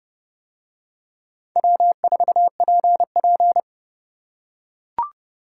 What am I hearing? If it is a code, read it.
Morse code, heard as a keyed tone, about 26 wpm, 700 Hz. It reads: W4PP